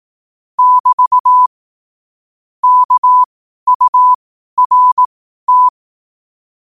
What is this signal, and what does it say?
Morse code, heard as a keyed tone, about 18 wpm, spaced as for 12 wpm, 1000 Hz. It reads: <BT> KURT
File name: 12wpm